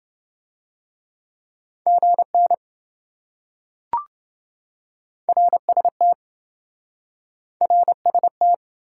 Morse code, heard as a keyed tone, about 27 wpm, 700 Hz. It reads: GN RST RST